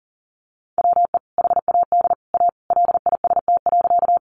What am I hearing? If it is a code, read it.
Morse code, heard as a keyed tone, about 40 wpm, 700 Hz. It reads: WE HAD A LIST.
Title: Sentences from Top 400 Words 40wpm